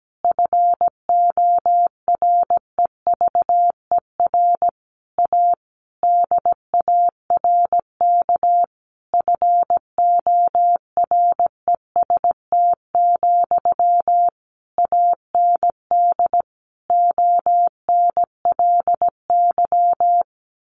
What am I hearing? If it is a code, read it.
Morse code, heard as a keyed tone, about 17 wpm, 700 Hz. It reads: FOREVER A DARK FOREST, AND ONLY
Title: A Sound Of Thunder 17wpm